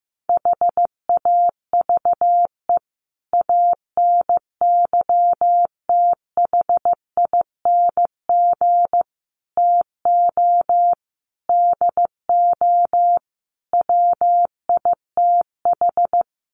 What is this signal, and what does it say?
Morse code, heard as a keyed tone, about 15 wpm, 700 Hz. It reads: HAVE ANYTHING TO DO WITH
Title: Sets of 5 Words 15wpm